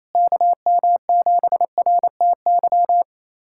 Morse code, heard as a keyed tone, about 28 wpm, 700 Hz. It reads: KM7RTY
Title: USA Callsign DE Callsign 28wpm